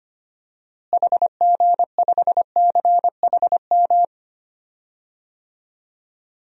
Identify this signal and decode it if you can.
Morse code, heard as a keyed tone, about 25 wpm, 700 Hz. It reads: HG5CHM
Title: DXCC-100 Call Signs - 6 Characters - Repeated 3x - 25wpm